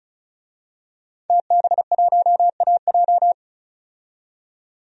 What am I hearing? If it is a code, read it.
Morse code, heard as a keyed tone, about 36 wpm, 700 Hz. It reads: TB1AJ
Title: DXCC-100 Call Signs - 5 Characters - 35wpm